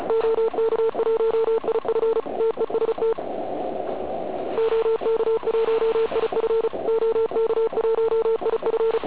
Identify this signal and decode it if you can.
Morse code, heard as a keyed tone, about 34 wpm, 450 Hz, with noise. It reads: OK1IF TEST OK1IF OK1IF
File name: Archiv / CUC106 / cuc106 PILEUP